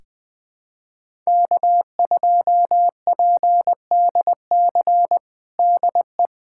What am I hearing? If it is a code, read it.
Morse code, heard as a keyed tone, about 20 wpm, 700 Hz. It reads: K2PDC DE
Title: USA Callsign DE Callsign - 5 Characters - 20wpm